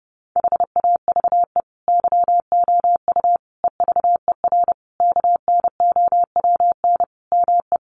Morse code, heard as a keyed tone, about 30 wpm, 700 Hz. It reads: HAVE YOU EVER KNOWN ME
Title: Sentences from Top 1400 Words 30wpm